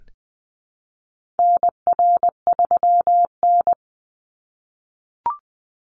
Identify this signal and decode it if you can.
Morse code, heard as a keyed tone, about 20 wpm, 700 Hz. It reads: NR3N
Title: Call Signs - 20wpm